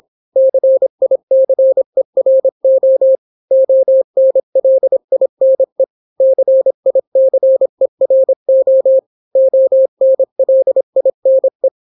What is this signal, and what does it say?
Morse code, heard as a keyed tone, about 26 wpm, 530 Hz. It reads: CICERO ONLINE CICERO ONLINE